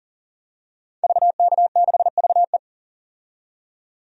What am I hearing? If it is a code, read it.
Morse code, heard as a keyed tone, about 40 wpm, 700 Hz. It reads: VK6VE
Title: DXCC-100 Call Signs - 5 Characters - 40wpm